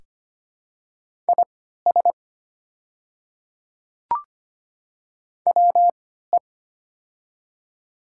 Morse code, heard as a keyed tone, about 25 wpm, spaced as for 14 wpm, 700 Hz. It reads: IS WE
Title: 3x Character Spacing - Top 100 Words - 25wpm